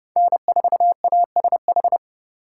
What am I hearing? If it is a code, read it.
Morse code, heard as a keyed tone, about 29 wpm, 700 Hz. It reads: N4ASH